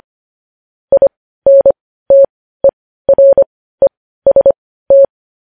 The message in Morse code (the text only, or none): INTEREST